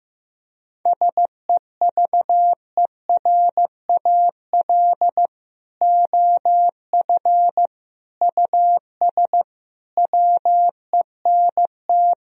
Morse code, heard as a keyed tone, about 15 wpm, 700 Hz. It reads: SEVERAL OF US WENT
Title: Sentences from Top 1000 Words - Review Part 2 - 15wpm